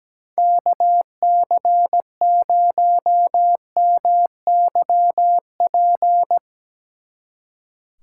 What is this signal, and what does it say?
Morse code, heard as a keyed tone, about 18 wpm, 700 Hz. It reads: KC0MYP
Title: Speed Racing - USA 2x3 Call Signs 17wpm